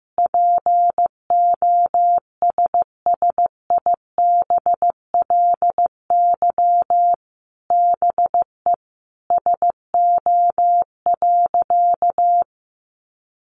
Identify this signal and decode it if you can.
Morse code, heard as a keyed tone, about 15 wpm, 700 Hz. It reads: POSSIBLY BE SO.
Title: Sentences from Top 1300 Words 15wpm